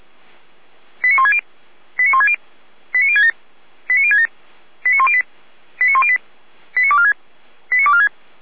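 Five-tone selective call (ZVEI), heard as quick runs of five tones, tones of about 80 ms. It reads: 89160 89160 89067 89067 89108 89108 89246 89246
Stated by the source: ZVEI five-tone call received with an SDR.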